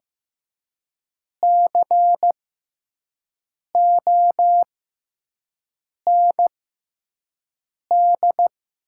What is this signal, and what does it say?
Morse code, heard as a keyed tone, about 15 wpm, spaced as for 5 wpm, 700 Hz. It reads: COND
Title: 6x Character Spacing - Top 500 Words - 15wpm